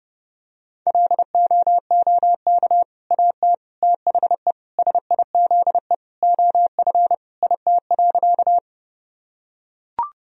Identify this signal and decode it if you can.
Morse code, heard as a keyed tone, about 30 wpm, 700 Hz. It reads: LOOK AT THE SIZE OF IT.
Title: Sentences from Top 1000 Words - Review Part 2 - 30wpm